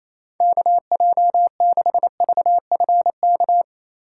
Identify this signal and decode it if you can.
Morse code, heard as a keyed tone, about 28 wpm, 700 Hz. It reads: KJ6VFK
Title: USA Call Signs - 6 Characters - 3x Character Spacing - 28wpm